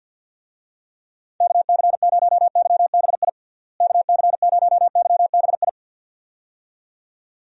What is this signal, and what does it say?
Morse code, heard as a keyed tone, about 47 wpm, 700 Hz. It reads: KC0YBI KC0YBI
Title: USA Call Signs - 6 Characters - Repeated 2x - 50wpm